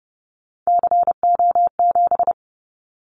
Morse code, heard as a keyed tone, about 30 wpm, 700 Hz. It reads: CO7
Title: Callsign Prefixes for Top 100 DXCC - 30wpm